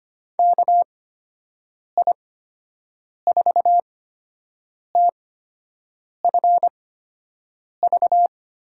Morse code, heard as a keyed tone, about 25 wpm, spaced as for 7 wpm, 700 Hz. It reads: KI4TFV
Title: USA Call Signs - 8x Character Spacing - 25wpm